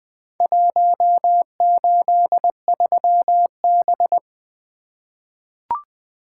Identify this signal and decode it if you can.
Morse code, heard as a keyed tone, about 20 wpm, 700 Hz. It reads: 183B
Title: Four Letter-Number Combinations - 20wpm